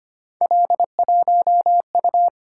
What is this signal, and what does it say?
Morse code, heard as a keyed tone, about 25 wpm, 700 Hz. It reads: L1U
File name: Three Letter-Number Combinations - 25wpm